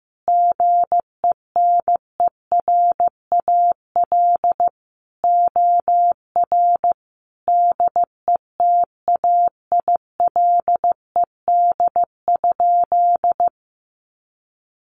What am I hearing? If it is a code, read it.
Morse code, heard as a keyed tone, about 15 wpm, 700 Hz. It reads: GENERAL OR DETAILED?
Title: Sentences from Top 1500 Words 15wpm